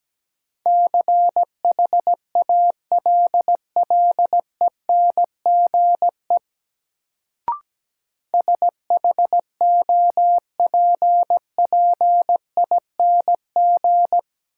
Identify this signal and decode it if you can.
Morse code, heard as a keyed tone, about 17 wpm, 700 Hz. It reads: CHALLENGE SHOPPING
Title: Sentences from Top 1000 Words - Review Part 7 - New Word Drill - 17wpm